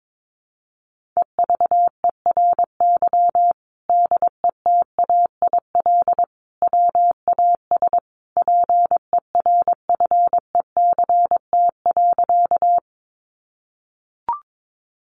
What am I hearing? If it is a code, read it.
Morse code, heard as a keyed tone, about 22 wpm, 700 Hz. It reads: EVERY DETAIL WAS PERFECT.
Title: Sentences from Top 1500 Words 22wpm